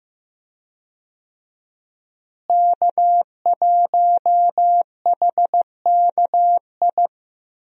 Morse code, heard as a keyed tone, about 15 wpm, 700 Hz. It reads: K1HKI
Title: USA Call Signs - 5 Characters - Repeated 2x - 15wpm